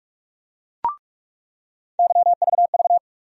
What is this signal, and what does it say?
Morse code, heard as a keyed tone, about 46 wpm, 700 Hz. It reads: YVV
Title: USA Call Sign Postfixes - 3 Letters 45wpm